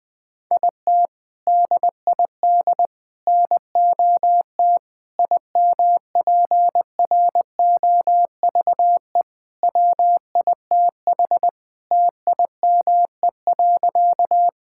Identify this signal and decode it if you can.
Morse code, heard as a keyed tone, about 20 wpm, 700 Hz. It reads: IT DID NOT IMPROVE WITH TIME.
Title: Sentences from Top 1000 Words - Review Part 5 - 20wpm